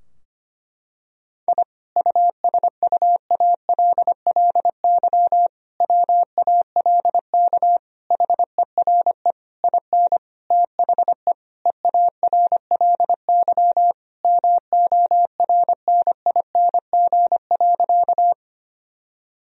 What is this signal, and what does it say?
Morse code, heard as a keyed tone, about 25 wpm, 700 Hz. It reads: I USUALLY WALK HERE IN THE EARLY MORNING.
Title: Sentences from Top 1000 Words - Review Part 3 - 25wpm